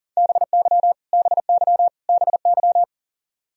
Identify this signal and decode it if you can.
Morse code, heard as a keyed tone, about 40 wpm, 700 Hz. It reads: BY BY BY
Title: Top 100 Words Repeated 40wpm